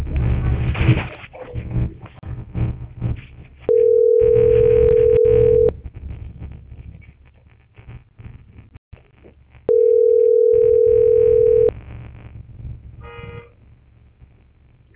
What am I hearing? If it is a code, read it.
Ringback tone (North America), 440 + 480 Hz.